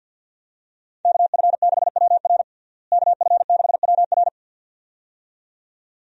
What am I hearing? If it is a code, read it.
Morse code, heard as a keyed tone, about 52 wpm, 700 Hz. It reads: KF6WR KF6WR
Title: USA Call Signs - 5 Characters - Repeated 2x - 50wpm